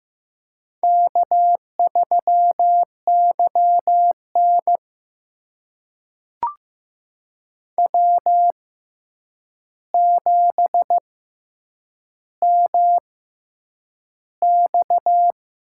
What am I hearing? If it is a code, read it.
Morse code, heard as a keyed tone, about 15 wpm, spaced as for 8 wpm, 700 Hz. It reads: K3YN W7MX
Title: USA Call Signs - 4 Characters - 6x Character Spacing - 15wpm